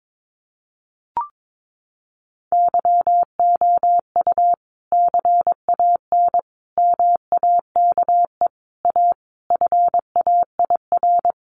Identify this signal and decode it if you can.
Morse code, heard as a keyed tone, about 22 wpm, 700 Hz. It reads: YOU CAN MAKE A FAIR
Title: Sentences from Top 1400 Words 22wpm